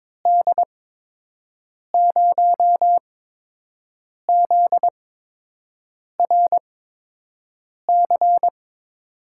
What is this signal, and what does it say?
Morse code, heard as a keyed tone, about 22 wpm, spaced as for 6 wpm, 700 Hz. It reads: D0ZRC
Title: USA Call Signs - 6 Characters - 8x Character Spacing - 22wpm